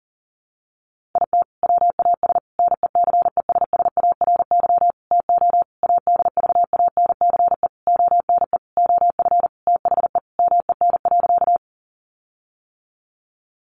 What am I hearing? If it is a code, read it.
Morse code, heard as a keyed tone, about 40 wpm, 700 Hz. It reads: IT WAS NECESSARY TO ADVANCE ONE OF THE MEN.
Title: Sentences from Top 1400 Words 40wpm